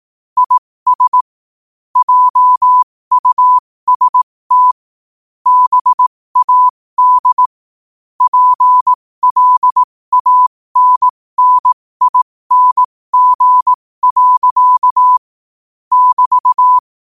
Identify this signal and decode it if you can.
Morse code, heard as a keyed tone, about 18 wpm, spaced as for 15 wpm, 1000 Hz. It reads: IS JUST BAD PLANNING. <BT>
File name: 15wpm